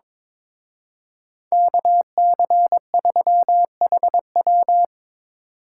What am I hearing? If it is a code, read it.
Morse code, heard as a keyed tone, about 26 wpm, 700 Hz. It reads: KC3HW
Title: Speed Racing - USA 2x2 Call Signs 22wpm